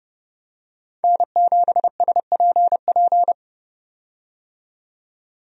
Morse code, heard as a keyed tone, about 30 wpm, 700 Hz. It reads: N7SPP
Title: USA Call Signs - 5 Characters - 30wpm